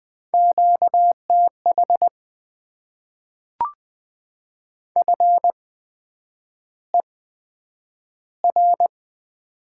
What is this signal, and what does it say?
Morse code, heard as a keyed tone, about 20 wpm, 700 Hz. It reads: Q T H F E R
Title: Top 20 QSO Elements - 8x Character Spacing - 20wpm